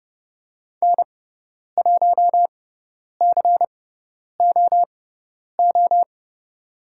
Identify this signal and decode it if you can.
Morse code, heard as a keyed tone, about 30 wpm, spaced as for 10 wpm, 700 Hz. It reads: N1COO
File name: Lesson 16g - Letter c - Call Signs 30wpm